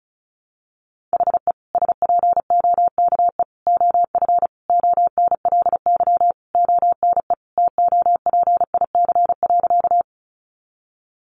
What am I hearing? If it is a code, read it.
Morse code, heard as a keyed tone, about 34 wpm, 700 Hz. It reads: HE SPOKE OF ONLY ONE TOPIC.
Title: Sentences from Top 1200 Words 35wpm